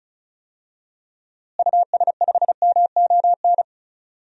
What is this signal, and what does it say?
Morse code, heard as a keyed tone, about 34 wpm, 700 Hz. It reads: US5MON
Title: DXCC-100 Call Signs - 6 Characters - Repeated 3x - 35wpm